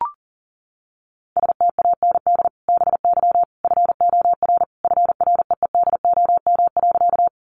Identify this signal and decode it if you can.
Morse code, heard as a keyed tone, about 40 wpm, 700 Hz. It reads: STAND BY FOR FREEDOM.
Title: Sentences from Top 800 Words 40wpm